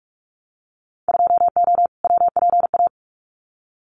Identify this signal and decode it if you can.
Morse code, heard as a keyed tone, about 46 wpm, 700 Hz. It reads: 2O WPA
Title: Winter Field Day Exchanges Encore 45wpm